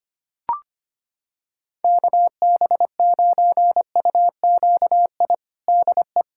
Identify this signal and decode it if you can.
Morse code, heard as a keyed tone, about 25 wpm, 700 Hz. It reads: KB9UQI DE
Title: USA Callsign DE Callsign - 6 Characters - 25wpm